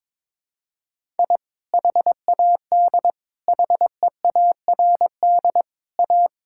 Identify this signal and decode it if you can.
Morse code, heard as a keyed tone, about 22 wpm, 700 Hz. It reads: I HAD HEARD A
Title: Sentences from Top 1000 Words - Review Part 6 - 22wpm